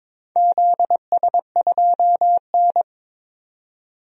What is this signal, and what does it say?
Morse code, heard as a keyed tone, about 22 wpm, 700 Hz. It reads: ZS2N
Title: DXCC-100 Call Signs - 4 Characters - 22wpm